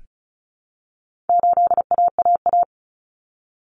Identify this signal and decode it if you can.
Morse code, heard as a keyed tone, about 34 wpm, 700 Hz. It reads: 8AAA
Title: Common Cut Numbers - 4-Digit 35wpm